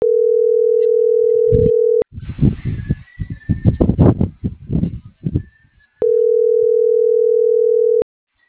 Ringback tone (North America), 440 + 480 Hz.